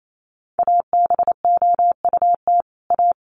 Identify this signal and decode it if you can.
Morse code, heard as a keyed tone, about 28 wpm, 700 Hz. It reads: ABOUT A
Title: Sets of 2 Words 28wpm